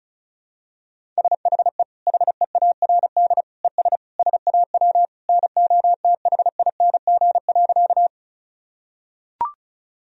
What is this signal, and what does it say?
Morse code, heard as a keyed tone, about 36 wpm, 700 Hz. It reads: SHE HEARD ES SAW NOTHING.
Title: Sentences from Top 500 Words and Abbreviations 35wpm